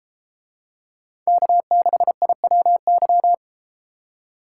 Morse code, heard as a keyed tone, about 31 wpm, 700 Hz. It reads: K6IWY